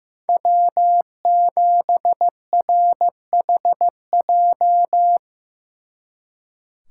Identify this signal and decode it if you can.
Morse code, heard as a keyed tone, about 16 wpm, 700 Hz. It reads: W7RHJ
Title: Speed Racing - USA 1x3 Call Signs 15wpm